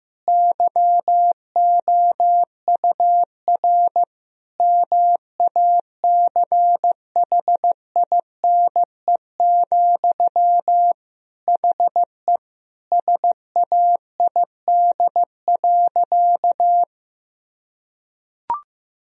Morse code, heard as a keyed tone, about 15 wpm, 700 Hz. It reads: YOUR MACHINE, HE SAID.
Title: Sentences from Top 1000 Words - Review Part 2 - 15wpm